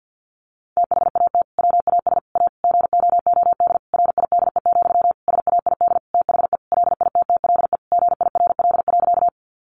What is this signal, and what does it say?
Morse code, heard as a keyed tone, about 50 wpm, 700 Hz. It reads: THAT WAS A GOOD RIDE, SAID THE LITTLE GIRL.
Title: Sentences from Top 1200 Words 50wpm